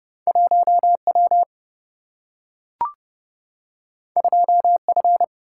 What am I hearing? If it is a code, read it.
Morse code, heard as a keyed tone, about 30 wpm, 700 Hz. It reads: 1W 2F
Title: Single-Letter Single-Number 30wpm